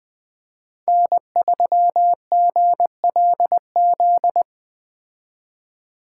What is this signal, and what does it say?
Morse code, heard as a keyed tone, about 21 wpm, 700 Hz. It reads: N3GLZ
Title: Speed Racing - USA Call Signs 17wpm